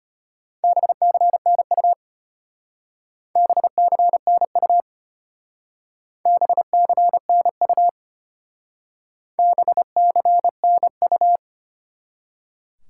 Morse code, heard as a keyed tone, about 31 wpm, 700 Hz. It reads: BCNU BCNU BCNU BCNU
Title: Speed-Racing Top 90 QSO Elements 25wpm